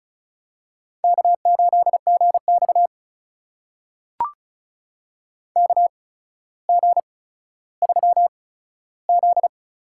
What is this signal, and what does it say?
Morse code, heard as a keyed tone, about 36 wpm, spaced as for 15 wpm, 700 Hz. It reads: K8GX KG3Z
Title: USA Call Signs - 8x Character Spacing - 35wpm